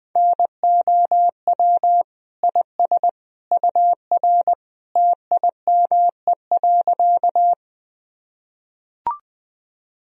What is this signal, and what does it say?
Morse code, heard as a keyed tone, about 20 wpm, 700 Hz. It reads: NOW IS UR TIME.
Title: Sentences from Top 100 Words and Abbreviations 20wpm